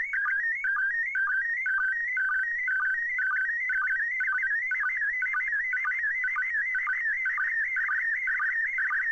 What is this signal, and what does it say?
SSTV (slow-scan television), PD 120.